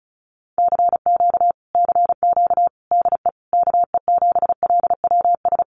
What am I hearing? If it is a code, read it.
Morse code, heard as a keyed tone, about 34 wpm, 700 Hz. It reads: CQ CQ DE KE7LWS
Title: Common QSO Phrases Encore 35wpm